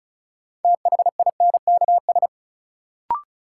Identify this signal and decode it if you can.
Morse code, heard as a keyed tone, about 36 wpm, 700 Hz. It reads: THINKS
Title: Sentences from Top 1000 Words - Review Part 6 - New Word Drill - 35wpm